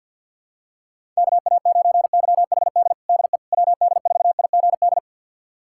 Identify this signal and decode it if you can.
Morse code, heard as a keyed tone, about 49 wpm, 700 Hz. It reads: KA9YHD DE WD4IGD
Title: USA Callsign DE Callsign 50wpm